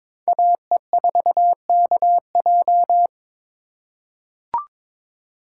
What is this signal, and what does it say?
Morse code, heard as a keyed tone, about 22 wpm, 700 Hz. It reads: AE4KJ
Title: USA Call Signs - 2x Character Spacing - 22wpm